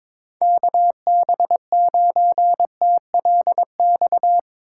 Morse code, heard as a keyed tone, about 25 wpm, 700 Hz. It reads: KB9TLX
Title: Speed Racing - USA 2x3 Call Signs 22wpm